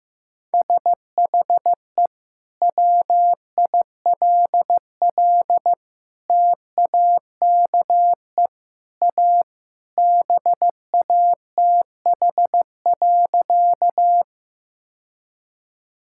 Morse code, heard as a keyed tone, about 15 wpm, 700 Hz. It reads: SHE WILL TAKE A BATH.
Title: Sentences from Top 1600 Words 15wpm